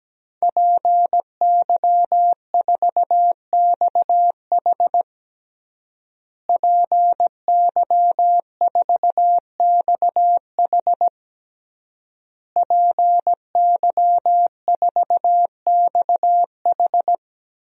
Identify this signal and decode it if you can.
Morse code, heard as a keyed tone, about 17 wpm, 700 Hz. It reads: PY4XH PY4XH PY4XH
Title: DXCC-100 Call Signs - 5 Characters - Repeated 3x - 17wpm